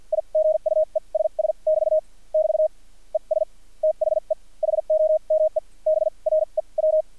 Morse code, heard as a keyed tone, about 50 wpm, spaced as for 37 wpm, 630 Hz. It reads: I GUESS <BT> <BT> ES THE HOMEBREW
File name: archived / CRSnet